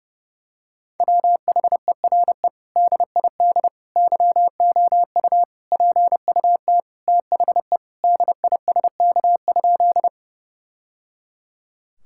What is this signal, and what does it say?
Morse code, heard as a keyed tone, about 32 wpm, 700 Hz. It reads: WHERE DID YOU PUT THE DISK?
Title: Sentences from Top 1000 Words - Review Part 6 - 30wpm